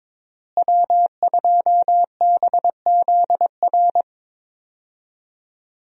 Morse code, heard as a keyed tone, about 22 wpm, 700 Hz. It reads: W2BZR
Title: USA Call Signs - 5 Characters - Repeated 3x - 22wpm